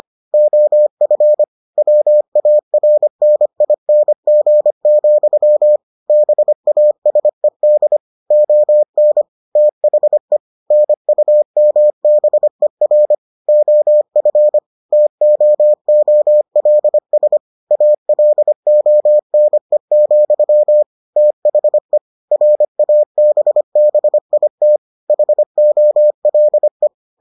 Morse code, heard as a keyed tone, about 25 wpm, 600 Hz. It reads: OF WARNING, BASED ON THE NUMBER OF TOOLS ALONE, THE RABBIT HOLE